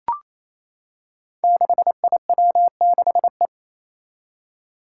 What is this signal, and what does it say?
Morse code, heard as a keyed tone, about 28 wpm, 700 Hz. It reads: 6IW6E
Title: Five Challenging Letter-Number Combinations 28wpm